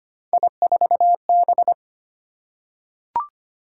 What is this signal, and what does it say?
Morse code, heard as a keyed tone, about 25 wpm, 700 Hz. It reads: I4B
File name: Three Letter-Number Combinations - 25wpm